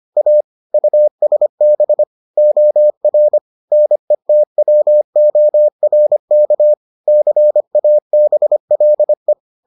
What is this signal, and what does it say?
Morse code, heard as a keyed tone, about 25 wpm, 600 Hz. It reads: A USB OR NETWORK CABLE